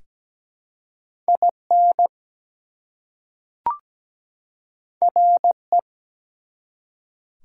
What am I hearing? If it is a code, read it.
Morse code, heard as a keyed tone, about 17 wpm, 700 Hz. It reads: IN RE
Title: Prefixes and Suffixes 17wpm